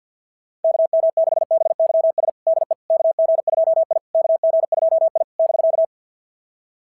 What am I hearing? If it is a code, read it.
Morse code, heard as a keyed tone, about 46 wpm, 660 Hz. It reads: KM6BYS DE KG2I KG2I <BK>
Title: SOTA QSO Practice 50wpm